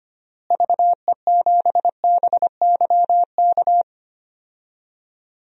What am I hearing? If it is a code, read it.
Morse code, heard as a keyed tone, about 25 wpm, 700 Hz. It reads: VE7BYK
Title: DXCC-100 Call Signs - 6 Characters - Repeated 2x - 25wpm